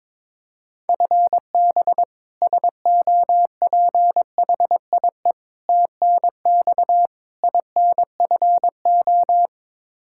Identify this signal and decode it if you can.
Morse code, heard as a keyed tone, about 22 wpm, 700 Hz. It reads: FB SOPHIE TNX INFO
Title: Practice QSO - Exchange 3 - More Info - 22wpm